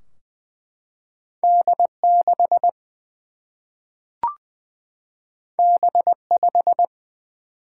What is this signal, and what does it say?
Morse code, heard as a keyed tone, about 20 wpm, 700 Hz. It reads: D 6 B 5
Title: Single-Letter Single-Number 20wpm